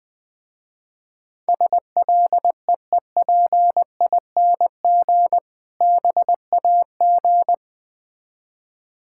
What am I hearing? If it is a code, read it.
Morse code, heard as a keyed tone, about 20 wpm, 690 Hz. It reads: SLEEPING BAG
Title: Difficult Words with Hints - 20wpm